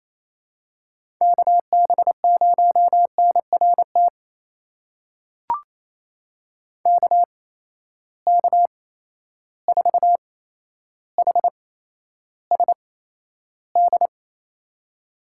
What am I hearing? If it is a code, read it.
Morse code, heard as a keyed tone, about 28 wpm, spaced as for 12 wpm, 700 Hz. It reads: KB0NRT KK4HSD
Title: USA Call Signs - 6 Characters - 8x Character Spacing - 28wpm